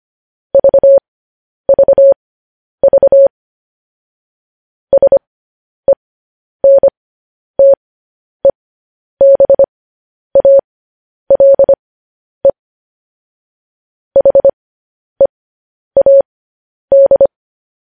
Morse code, heard as a keyed tone, about 25 wpm, spaced as for 10 wpm, 550 Hz. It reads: VVV SENTEBALE HEAD
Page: news-headlines-in-morse-code-at-10-wpm